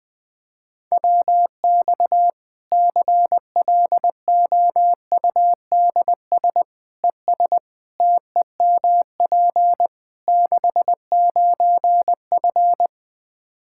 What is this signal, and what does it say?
Morse code, heard as a keyed tone, about 20 wpm, 700 Hz. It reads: WX CLOUDS ES TEMP 69F
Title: K1USN SST QSO Practice 20wpm